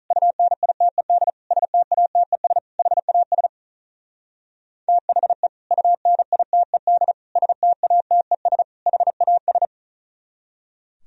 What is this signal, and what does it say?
Morse code, heard as a keyed tone, about 44 wpm, 700 Hz. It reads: UNITED STATES HAS THE UNITED STATES HAS